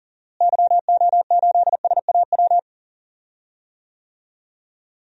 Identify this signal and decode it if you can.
Morse code, heard as a keyed tone, about 40 wpm, 700 Hz. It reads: YO8SAW
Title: DXCC-100 Call Signs - 6 Characters - Repeated 3x - 40wpm